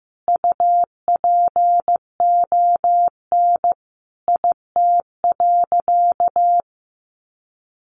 Morse code, heard as a keyed tone, about 15 wpm, 700 Hz. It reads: UPON IT.
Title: Sentences from Top 1300 Words 15wpm